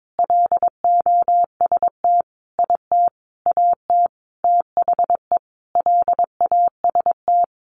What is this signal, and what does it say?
Morse code, heard as a keyed tone, about 22 wpm, 700 Hz. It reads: LOST IT AT THE LAST
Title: Sentences from Top 1400 Words 22wpm